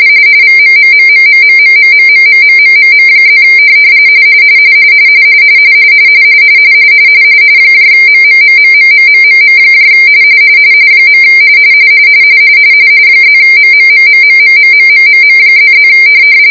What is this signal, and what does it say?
RTTY, about 45.45 baud. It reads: VMMMMMMMMMMMMMMM+ MMMMMMMMM . +MMMMMMMMMMM++